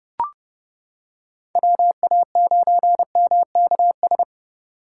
Morse code, heard as a keyed tone, about 29 wpm, 700 Hz. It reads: WA9MKS